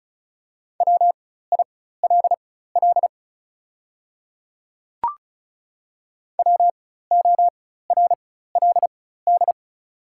Morse code, heard as a keyed tone, about 36 wpm, spaced as for 16 wpm, 690 Hz. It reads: WILL WORLD
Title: 4x Character Spacing - Top 300 Words - 35wpm